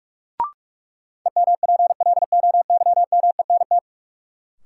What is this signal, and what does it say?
Morse code, heard as a keyed tone, about 45 wpm, 700 Hz. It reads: EMPLOYMENT
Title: Sentences from Top 1500 Words - New Word Drill Alt - 45wpm